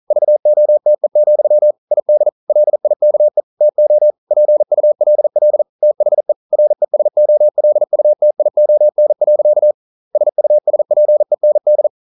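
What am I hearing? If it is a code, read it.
Morse code, heard as a keyed tone, about 40 wpm, 600 Hz. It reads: VOTE, ID LIKE TO PULL THE RESOLUTION. SUSPEND
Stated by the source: Morse code practice at 41 wpm, from CW med Gnister.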